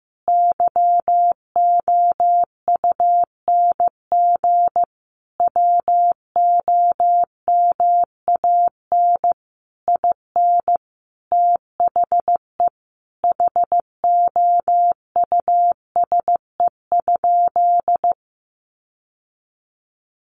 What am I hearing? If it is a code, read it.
Morse code, heard as a keyed tone, about 15 wpm, 700 Hz. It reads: YOUNG WOMAN IN THE HOUSE?
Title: Sentences from Top 1100 Words 15wpm